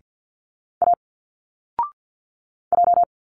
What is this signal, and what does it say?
Morse code, heard as a keyed tone, about 50 wpm, 700 Hz. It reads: A J
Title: Single Letters 50wpm